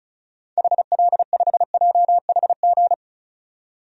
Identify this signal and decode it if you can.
Morse code, heard as a keyed tone, about 36 wpm, 690 Hz. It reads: HL5JHG
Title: DXCC-100 Call Signs - 6 Characters - 35wpm